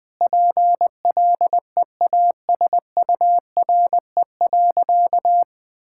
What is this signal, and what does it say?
Morse code, heard as a keyed tone, about 20 wpm, 700 Hz. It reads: PLEASURE.
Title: Sentences from Top 1600 Words 20wpm